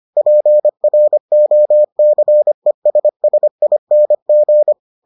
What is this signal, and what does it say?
Morse code, heard as a keyed tone, about 25 wpm, 600 Hz. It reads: PROCESSING